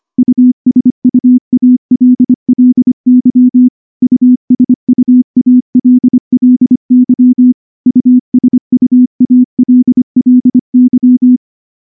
Morse code, heard as a keyed tone, about 24 wpm, 250 Hz. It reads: USUALLY USUALLY USUALLY
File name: morse-training-25WPM